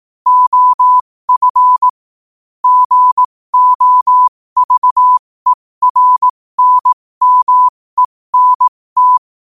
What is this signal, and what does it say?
Morse code, heard as a keyed tone, about 18 wpm, spaced as for 15 wpm, 1000 Hz. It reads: OF GOVERNMENT
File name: qotdfiles / 15wpm